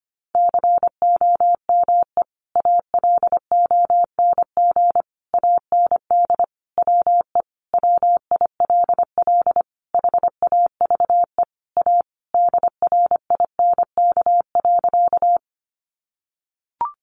Morse code, heard as a keyed tone, about 25 wpm, 700 Hz. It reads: COME ALONG AND WE WILL HAVE A DRINK.
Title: Sentences from Top 1500 Words 25wpm